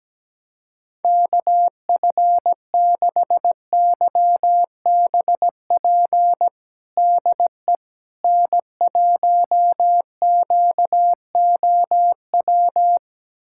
Morse code, heard as a keyed tone, about 17 wpm, 700 Hz. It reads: KF6YBP DE N1QOW
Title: USA Callsign DE Callsign 17wpm